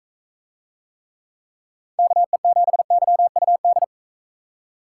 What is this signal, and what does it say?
Morse code, heard as a keyed tone, about 39 wpm, 700 Hz. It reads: KE7YUD